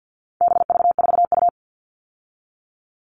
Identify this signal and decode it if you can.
Morse code, heard as a keyed tone, about 50 wpm, 700 Hz. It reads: BV4U
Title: All Cut Numbers - 4-Digit 50wpm